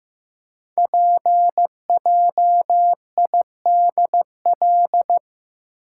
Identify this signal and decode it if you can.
Morse code, heard as a keyed tone, about 15 wpm, 700 Hz. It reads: PJIDL
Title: Code Groups - 5 Letters - 15wpm